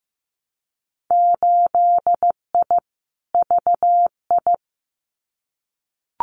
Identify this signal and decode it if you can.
Morse code, heard as a keyed tone, about 15 wpm, 700 Hz. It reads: 8I VI
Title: Winter Field Day Exchanges Encore 15wpm